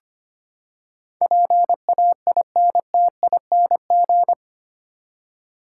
Morse code, heard as a keyed tone, about 25 wpm, 700 Hz. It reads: PAINTING
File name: Eight Letter Words - 25wpm